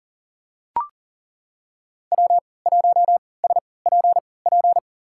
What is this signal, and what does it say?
Morse code, heard as a keyed tone, about 37 wpm, spaced as for 22 wpm, 700 Hz. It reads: W1SPP
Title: USA Call Signs - 5 Characters - 3x Character Spacing - 40wpm